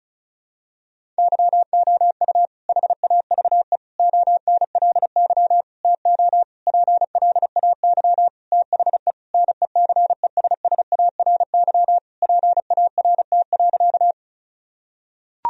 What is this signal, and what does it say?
Morse code, heard as a keyed tone, about 36 wpm, 700 Hz. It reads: YOU HAVE ONLY TO PLAY THE NECESSARY PART.
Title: Sentences from Top 1000 Words - Review Part 1 - 35wpm